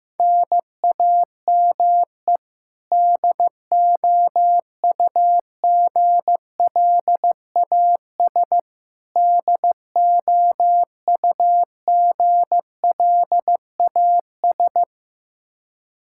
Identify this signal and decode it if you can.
Morse code, heard as a keyed tone, about 16 wpm, 700 Hz. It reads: NAME DOUGLAS DOUGLAS
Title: Practice QSO - Exchange 1 - Beginning - 15wpm